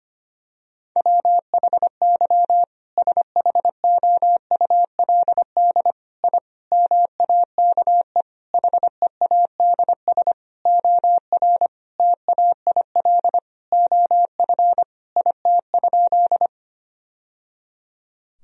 Morse code, heard as a keyed tone, about 25 wpm, 700 Hz. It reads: WHY SHOULD I MAKE HEADS OR TAIL OF IT?
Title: Sentences from Top 1000 Words - Review Part 3 - 25wpm